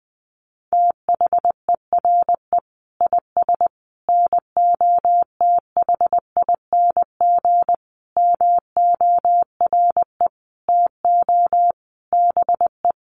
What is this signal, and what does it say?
Morse code, heard as a keyed tone, about 20 wpm, 700 Hz. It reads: THERE IS NOTHING MORE TO BE
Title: Sentences from Top 500 Words 20wpm